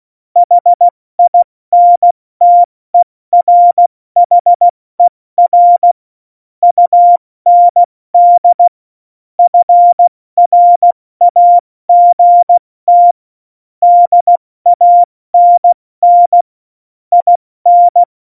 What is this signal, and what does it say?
Morse code, heard as a keyed tone, about 16 wpm, 700 Hz. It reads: HINTERHER UND FRAGT DANN IN
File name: Morsekurs / Witze / Witz 18. September 2018, 70 F80